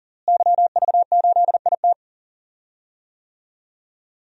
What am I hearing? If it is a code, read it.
Morse code, heard as a keyed tone, about 40 wpm, 700 Hz. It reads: YV8IT
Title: DXCC-100 Call Signs - 5 Characters - Repeated 3x - 40wpm